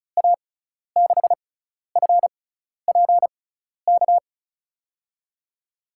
Morse code, heard as a keyed tone, about 36 wpm, spaced as for 12 wpm, 700 Hz. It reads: A6FPK
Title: USA Call Signs - 6 Characters - 6x Character Spacing - 35wpm